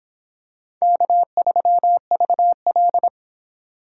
Morse code, heard as a keyed tone, about 26 wpm, 700 Hz. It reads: K3VL